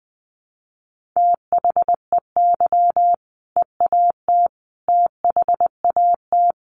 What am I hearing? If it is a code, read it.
Morse code, heard as a keyed tone, about 19 wpm, 700 Hz. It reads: THEY EAT THAT
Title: Sentences from Top 1400 Words 20wpm